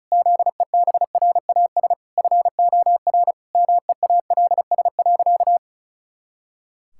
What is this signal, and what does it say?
Morse code, heard as a keyed tone, about 34 wpm, 700 Hz. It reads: ZEBRAS FOR MEALS.
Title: Pangram Challenge v2 - 35wpm